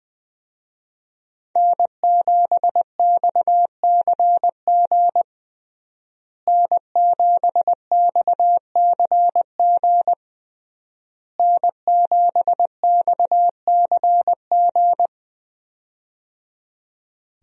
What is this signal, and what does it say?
Morse code, heard as a keyed tone, about 20 wpm, 700 Hz. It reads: N7XCG N7XCG N7XCG
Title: USA Call Signs - 5 Characters - Repeated 3x - 20wpm